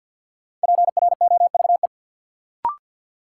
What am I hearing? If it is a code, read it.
Morse code, heard as a keyed tone, about 46 wpm, 700 Hz. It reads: PROVE
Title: Sentences from Top 1600 Words - New Word Drill Alt - 50wpm